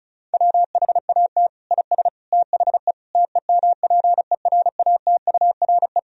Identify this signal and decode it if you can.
Morse code, heard as a keyed tone, about 36 wpm, 700 Hz. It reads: WHAT IS THE TEMPERATURE
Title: Sentences from Top 1000 Words - Review Part 4 - 35wpm